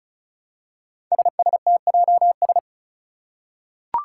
Morse code, heard as a keyed tone, about 36 wpm, 700 Hz. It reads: SSTJS
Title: Five Challenging Letter-Number Combinations 35wpm